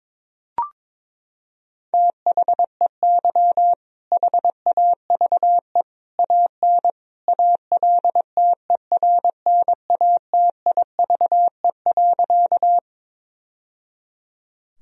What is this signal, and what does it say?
Morse code, heard as a keyed tone, about 22 wpm, 700 Hz. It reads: THEY HAVE AN ALTERNATIVE.
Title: Sentences from Top 1000 Words - Review Part 4 - 22wpm